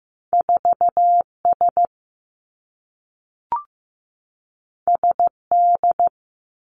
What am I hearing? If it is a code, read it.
Morse code, heard as a keyed tone, about 15 wpm, 700 Hz. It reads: 4S SD
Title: Two Challenging Letter-Number Combinations 15wpm